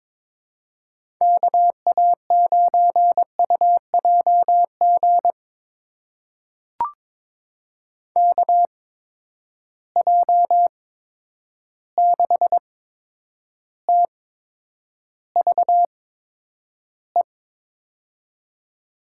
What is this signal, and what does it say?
Morse code, heard as a keyed tone, about 22 wpm, spaced as for 9 wpm, 700 Hz. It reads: KA9UJG KJ6TVE
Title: USA Call Signs - 6 Characters - 8x Character Spacing - 22wpm